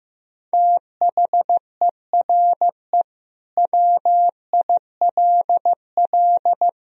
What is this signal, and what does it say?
Morse code, heard as a keyed tone, about 15 wpm, 700 Hz. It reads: THERE WILL
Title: Sentences from Top 1000 Words - Review Part 4 - 15wpm